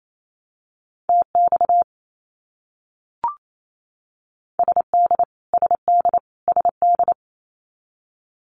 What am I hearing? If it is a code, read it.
Morse code, heard as a keyed tone, about 28 wpm, 700 Hz. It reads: TX SD SD SD
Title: US States Repeated 3x 28wpm